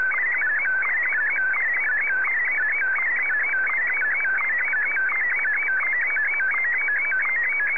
SSTV (slow-scan television), PD 90.